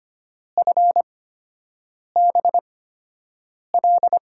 Morse code, heard as a keyed tone, about 25 wpm, spaced as for 7 wpm, 700 Hz. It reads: FBL
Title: USA Call Signs - 6 Characters - 8x Character Spacing - 25wpm